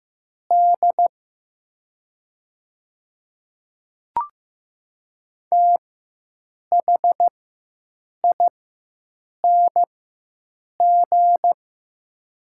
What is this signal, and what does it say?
Morse code, heard as a keyed tone, about 15 wpm, spaced as for 7 wpm, 700 Hz. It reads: D THING
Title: 4x Character Spacing - Top 100 Words - 15wpm